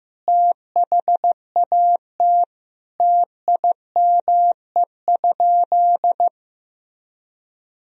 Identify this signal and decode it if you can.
Morse code, heard as a keyed tone, about 15 wpm, 700 Hz. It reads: THAT TIME?
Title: Sentences from Top 200 Words and Abbreviations 15wpm